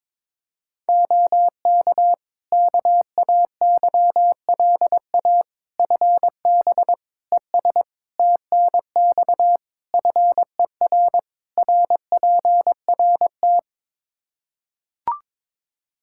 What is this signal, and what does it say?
Morse code, heard as a keyed tone, about 22 wpm, 700 Hz. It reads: OK KAYLA FB ES TNX FER RPRT
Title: K1USN SST QSO Practice 22wpm